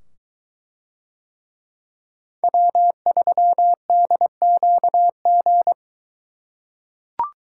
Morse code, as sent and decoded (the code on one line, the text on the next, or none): .-- ...-- -.. --.- --.
W3DQG